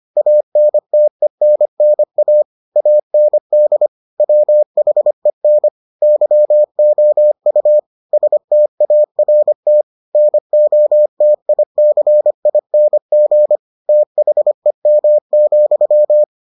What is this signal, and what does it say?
Morse code, heard as a keyed tone, about 25 wpm, 600 Hz. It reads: ANTENNA AND WHEN YOU START NOTICING THEM,